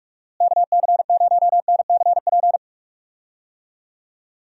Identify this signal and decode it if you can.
Morse code, heard as a keyed tone, about 47 wpm, 700 Hz. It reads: KC0NCP
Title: USA Call Signs - 6 Characters - Repeated 3x - 45wpm